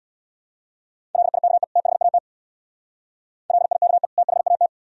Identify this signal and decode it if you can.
Morse code, heard as a keyed tone, about 59 wpm, 700 Hz. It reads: LIKE THAT LIKE THAT
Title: Speed-Racing - Sets of 2 Words 50wpm